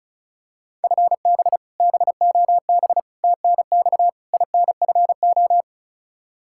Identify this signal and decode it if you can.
Morse code, heard as a keyed tone, about 36 wpm, 700 Hz. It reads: FB BOB TNX INFO
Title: Practice QSO - Exchange 3 - More Info - 35wpm